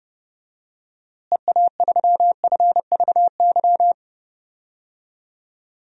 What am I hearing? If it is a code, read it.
Morse code, heard as a keyed tone, about 31 wpm, 700 Hz. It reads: EA3FVY
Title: DXCC-100 Call Signs - 6 Characters - Repeated 3x - 30wpm